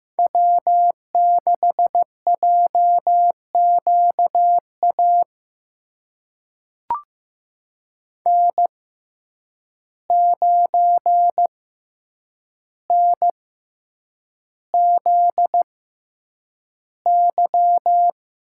Morse code, heard as a keyed tone, about 15 wpm, spaced as for 8 wpm, 700 Hz. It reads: W6JQA N9NZY
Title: USA Call Signs - 5 Characters - 6x Character Spacing - 15wpm